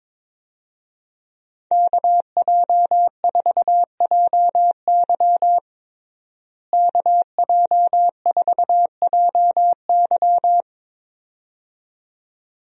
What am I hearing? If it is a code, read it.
Morse code, heard as a keyed tone, about 22 wpm, 700 Hz. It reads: KJ4JY KJ4JY
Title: USA Call Signs - 5 Characters - Repeated 2x - 22wpm